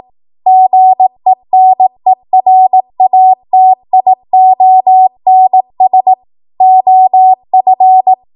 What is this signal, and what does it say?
Morse code, heard as a keyed tone, about 18 wpm, 750 Hz. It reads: GENERATIONS OF
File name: Morse / 18WPM